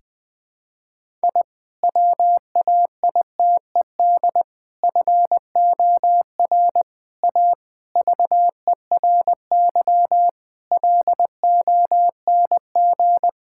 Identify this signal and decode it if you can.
Morse code, heard as a keyed tone, about 21 wpm, 700 Hz. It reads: I WAITED FOR A VERY LONG
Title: Sentences with 5-Word Phrases - 20wpm